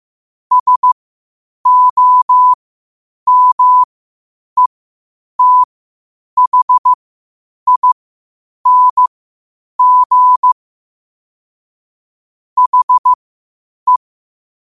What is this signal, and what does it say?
Morse code, heard as a keyed tone, about 15 wpm, spaced as for 8 wpm, 1000 Hz. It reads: SOMETHING HE
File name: qotdfiles / 8wpm